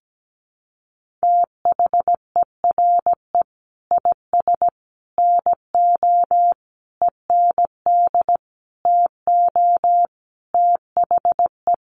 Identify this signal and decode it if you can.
Morse code, heard as a keyed tone, about 17 wpm, 700 Hz. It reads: THERE IS NO END TO THE
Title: Sentences from Top 1200 Words 17wpm